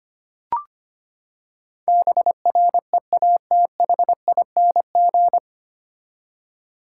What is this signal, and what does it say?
Morse code, heard as a keyed tone, about 25 wpm, 700 Hz. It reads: BREATHING
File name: Nine Letter Words - 25wpm